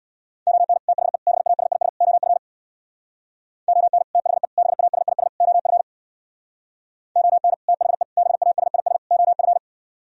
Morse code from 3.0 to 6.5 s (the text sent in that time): ON THE BASIS OF